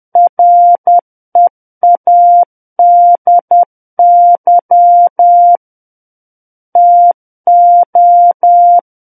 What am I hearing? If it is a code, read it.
Morse code, heard as a keyed tone, about 10 wpm, 700 Hz. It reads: READY TO
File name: Book One / 01-Chapter 1 - 10 WPM